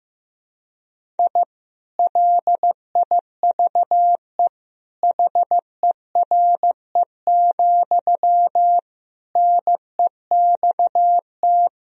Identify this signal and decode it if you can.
Morse code, heard as a keyed tone, about 15 wpm, 700 Hz. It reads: I LIVE HERE, NEXT
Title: Sentences from Top 1000 Words - Review Part 1 - 15wpm